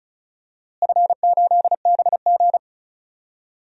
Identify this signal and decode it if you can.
Morse code, heard as a keyed tone, about 34 wpm, 700 Hz. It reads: F8BG
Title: DXCC-100 Call Signs - 4 Characters - Repeated 2x - 35wpm